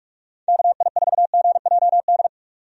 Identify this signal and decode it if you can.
Morse code, heard as a keyed tone, about 46 wpm, 700 Hz. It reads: KI4GJD
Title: USA Call Signs - 6 Characters - 45wpm